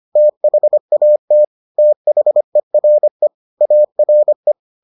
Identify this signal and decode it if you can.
Morse code, heard as a keyed tone, about 25 wpm, 600 Hz. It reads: THAT THERE ARE